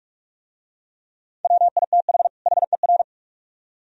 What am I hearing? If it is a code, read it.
Morse code, heard as a keyed tone, about 43 wpm, 700 Hz. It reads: WITH HER